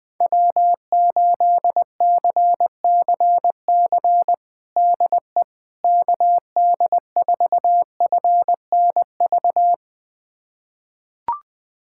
Morse code, heard as a keyed tone, about 21 wpm, 700 Hz. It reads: W8CCC DE KD4FNV
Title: USA Callsign DE Callsign 20wpm